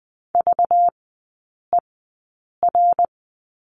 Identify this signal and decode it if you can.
Morse code, heard as a keyed tone, about 20 wpm, spaced as for 8 wpm, 690 Hz. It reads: VER
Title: 191 Lesson 31e - Letter v - Three Characters 20wpm